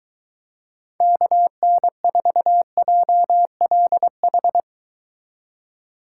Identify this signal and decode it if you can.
Morse code, heard as a keyed tone, about 23 wpm, 700 Hz. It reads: KN4JLH